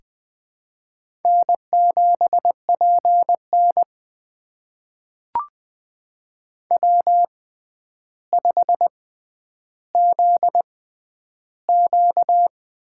Morse code, heard as a keyed tone, about 20 wpm, spaced as for 10 wpm, 700 Hz. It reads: N7PN W5ZQ
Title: USA Call Signs - 6x Character Spacing - 20wpm